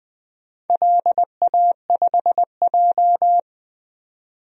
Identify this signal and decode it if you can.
Morse code, heard as a keyed tone, about 20 wpm, 700 Hz. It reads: LA5J
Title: DXCC-100 Call Signs - 4 Characters - 20wpm